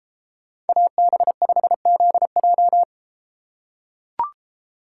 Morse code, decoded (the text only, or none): AB5ZJ